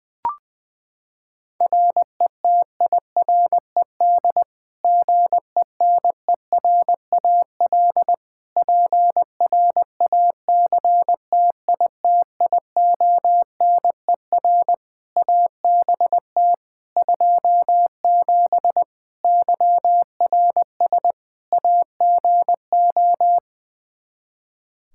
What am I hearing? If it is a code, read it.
Morse code, heard as a keyed tone, about 21 wpm, 700 Hz. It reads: RETIRED GENERAL PRACTITIONER ABT 27 YRS AGO
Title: Practice QSO - Exchange 3 - More Info - 20wpm